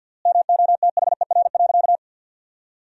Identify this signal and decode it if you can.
Morse code, heard as a keyed tone, about 52 wpm, 700 Hz. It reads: MOTHER.
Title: 3x Word-Spacing - Sentences from Top 200 Words - 50wpm